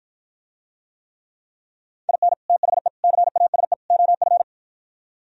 Morse code, heard as a keyed tone, about 53 wpm, 700 Hz. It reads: IN THE CASE OF